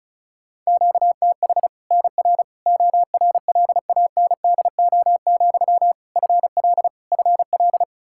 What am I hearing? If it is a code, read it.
Morse code, heard as a keyed tone, about 36 wpm, 700 Hz. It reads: QTH NR ORLANDO, FL FL
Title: Practice QSO - Exchange 1 - Beginning - 35wpm